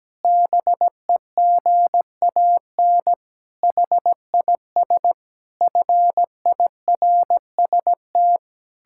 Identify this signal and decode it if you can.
Morse code, heard as a keyed tone, about 17 wpm, 700 Hz. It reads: BEGAN HIS FIRST
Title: Pangram Challenge v2 - 17wpm